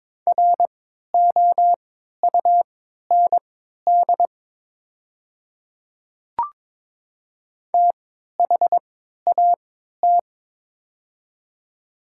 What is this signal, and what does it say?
Morse code, heard as a keyed tone, about 22 wpm, spaced as for 12 wpm, 700 Hz. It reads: ROUND THAT
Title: 3x Character Spacing - Top 200 Words - 22wpm